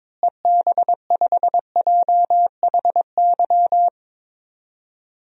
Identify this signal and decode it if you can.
Morse code, heard as a keyed tone, about 22 wpm, 700 Hz. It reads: EB5JHY
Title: DXCC-100 Call Signs - 6 Characters - Repeated 3x - 22wpm